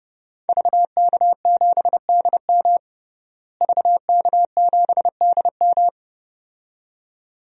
Morse code, heard as a keyed tone, about 32 wpm, 700 Hz. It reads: VK7DM VK7DM
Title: DXCC-100 Call Signs - 5 Characters - Repeated 2x - 30wpm